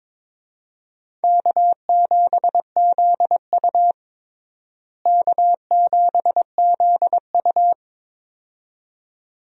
Morse code, heard as a keyed tone, about 21 wpm, 700 Hz. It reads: K7ZU K7ZU
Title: USA Call Signs - 4 Characters - Repeated 2x - 22wpm